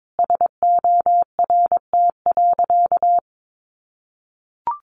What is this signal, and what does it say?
Morse code, heard as a keyed tone, about 22 wpm, 700 Hz. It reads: SORT.
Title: Sentences from Top 1500 Words 22wpm